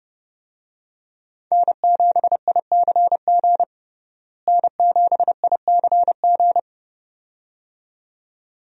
Morse code, heard as a keyed tone, about 30 wpm, 700 Hz. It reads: N7ICG N7ICG
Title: USA Call Signs - 5 Characters - Repeated 2x - 30wpm